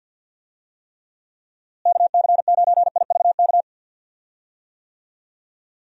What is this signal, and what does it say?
Morse code, heard as a keyed tone, about 52 wpm, 700 Hz. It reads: KC9IVK
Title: USA Call Signs - 6 Characters - Repeated 3x - 50wpm